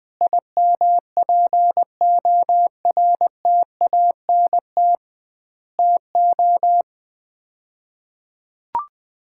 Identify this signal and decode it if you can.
Morse code, heard as a keyed tone, about 21 wpm, 700 Hz. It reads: IMPORTANT TO
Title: 2x Word-Spacing - Sets of 4 Words - 20wpm